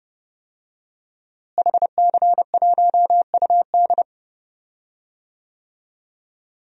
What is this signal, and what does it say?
Morse code, heard as a keyed tone, about 30 wpm, 700 Hz. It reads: HC1UD
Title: DXCC-100 Call Signs - 5 Characters - Repeated 3x - 30wpm